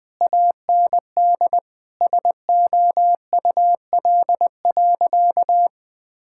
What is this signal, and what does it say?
Morse code, heard as a keyed tone, about 20 wpm, 700 Hz. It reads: AND SOUL.
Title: Sentences with 9-Word Phrases - 20wpm